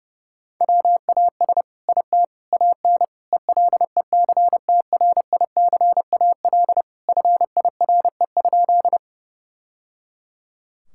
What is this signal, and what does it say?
Morse code, heard as a keyed tone, about 30 wpm, 700 Hz. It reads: WAS IT AN ELECTRICAL FIRE?
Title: Sentences from Top 1000 Words - Review Part 6 - 30wpm